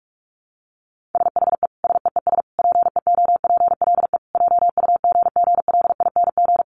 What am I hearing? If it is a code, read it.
Morse code, heard as a keyed tone, about 44 wpm, 700 Hz. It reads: SHE SEES PEOPLE JUGGLING
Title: Easy Phrases 45wpm